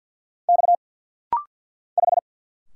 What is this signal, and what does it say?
Morse code, heard as a keyed tone, about 50 wpm, 700 Hz. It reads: X 5
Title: Single Letter-Number - Rapid-Fire 50wpm